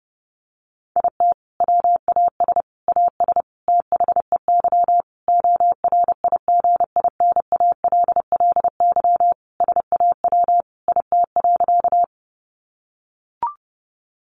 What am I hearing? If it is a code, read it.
Morse code, heard as a keyed tone, about 30 wpm, 700 Hz. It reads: IT WAS AS THEY ORIGINALLY SAW IT.
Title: Sentences from Top 1500 Words 30wpm